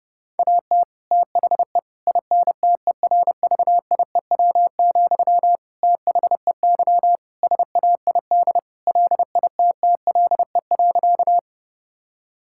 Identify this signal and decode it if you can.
Morse code, heard as a keyed tone, about 30 wpm, 700 Hz. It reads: AT THE INTERVIEW, THEY SAID LITTLE.
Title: Sentences from Top 1600 Words 30wpm